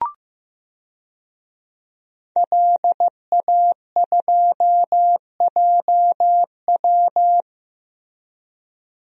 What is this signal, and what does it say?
Morse code, heard as a keyed tone, about 15 wpm, 700 Hz. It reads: LA2JW
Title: DXCC-100 Call Signs - 5 Characters - Repeated 2x - 15wpm